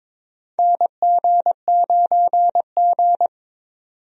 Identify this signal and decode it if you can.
Morse code, heard as a keyed tone, about 22 wpm, 700 Hz. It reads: NG9G
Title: USA Call Signs - 4 Characters - 22wpm